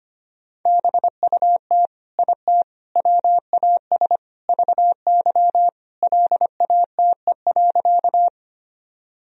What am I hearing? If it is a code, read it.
Morse code, heard as a keyed tone, about 25 wpm, 700 Hz. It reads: BUT IT WAS VY LATE.
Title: Sentences from Top 300 Words and Abbreviations 25wpm